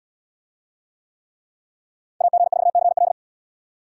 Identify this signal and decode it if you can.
Morse code, heard as a keyed tone, about 70 wpm, 700 Hz. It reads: AG4CL